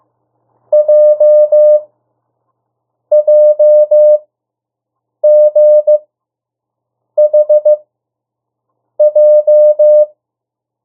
Morse code, heard as a keyed tone, about 15 wpm, spaced as for 6 wpm, 600 Hz. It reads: JJGHJ
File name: urok 1